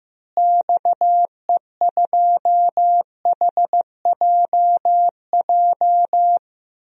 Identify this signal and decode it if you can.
Morse code, heard as a keyed tone, about 15 wpm, 700 Hz. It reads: XE2HJJ
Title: DXCC-100 Call Signs - 6 Characters - Repeated 3x - 15wpm